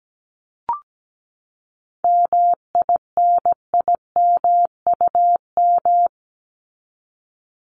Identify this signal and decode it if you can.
Morse code, heard as a keyed tone, about 17 wpm, 700 Hz. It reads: MINIMUM
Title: Top 1251-1500 Words 17wpm